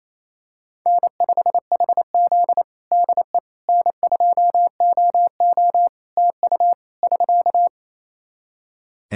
Morse code, heard as a keyed tone, about 28 wpm, 700 Hz. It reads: N5HZ DE N2OO TU <SK>
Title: Practice QSO - Exchange 4 - Ending - 28wpm